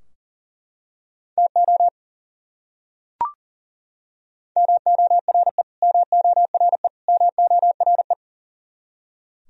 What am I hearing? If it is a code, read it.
Morse code, heard as a keyed tone, about 40 wpm, 700 Hz. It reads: TO MORE MORE MORE
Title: Top 100 Words Repeated 40wpm